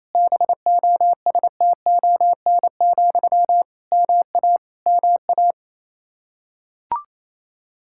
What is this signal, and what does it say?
Morse code, heard as a keyed tone, about 28 wpm, 700 Hz. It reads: BOSTON, MA MA
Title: Practice QSO - Exchange 1 - Beginning - 28wpm